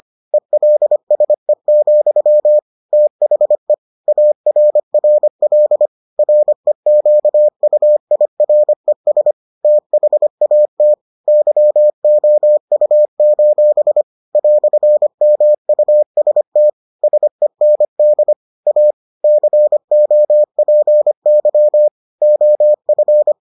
Morse code, heard as a keyed tone, about 25 wpm, 600 Hz. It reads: ELSE, THE ARRL REQUIRES THAT YOU: "MUST SEND A COPY OF